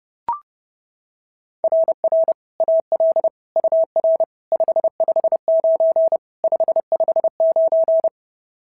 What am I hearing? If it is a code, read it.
Morse code, heard as a keyed tone, about 26 wpm, 650 Hz. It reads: RR AL UR 559 559
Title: SOTA QSO Practice 30wpm